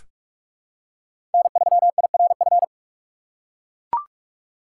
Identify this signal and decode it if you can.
Morse code, heard as a keyed tone, about 41 wpm, 700 Hz. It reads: N3IRF
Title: USA Call Signs - 5 Characters - 2x Character Spacing - 45wpm